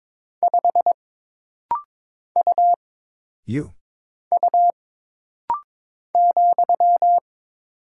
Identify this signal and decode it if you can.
Morse code, heard as a keyed tone, about 22 wpm, 700 Hz. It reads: 5 U U ,
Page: This rapid-fire timing is geared to push your ICR (Instant Character Recognition) skills.